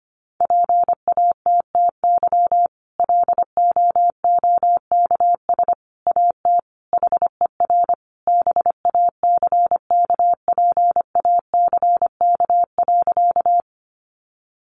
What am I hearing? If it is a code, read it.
Morse code, heard as a keyed tone, about 25 wpm, 700 Hz. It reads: PATTY LOOKS AT HER BACKPACK.
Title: Additional Easy Phrases 25wpm